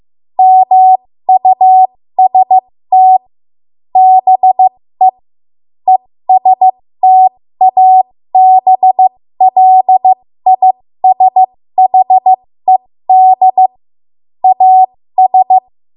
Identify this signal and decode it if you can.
Morse code, heard as a keyed tone, about 15 wpm, spaced as for 13 wpm, 750 Hz. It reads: MUST BE ESTABLISHED AS
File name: Morse / 13WPM